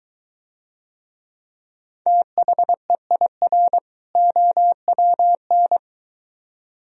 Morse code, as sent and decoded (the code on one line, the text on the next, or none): - .... .. .-. / --- .-- -.
THEIR OWN